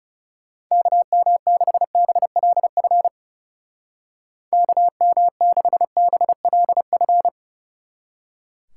Morse code, heard as a keyed tone, about 36 wpm, 700 Hz. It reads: KM6BLF KM6BLF
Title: Speed Racing - USA 2x3 Call Signs 30wpm